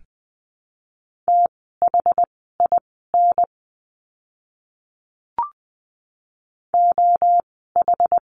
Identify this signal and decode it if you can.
Morse code, heard as a keyed tone, about 20 wpm, spaced as for 14 wpm, 700 Hz. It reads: THIN OH
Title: 068 Lesson 12f - Letter d - Words 20wpm